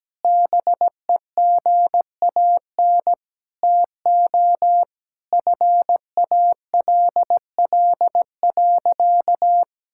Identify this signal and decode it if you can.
Morse code, heard as a keyed tone, about 17 wpm, 700 Hz. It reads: BEGAN TO FALL.
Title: Sentences from Top 1000 Words - Review Part 6 - 17wpm